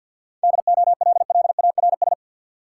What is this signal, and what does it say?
Morse code, heard as a keyed tone, about 53 wpm, 700 Hz. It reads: DOLLARS
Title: Sets of 5 Words - New Word Drill - 50wpm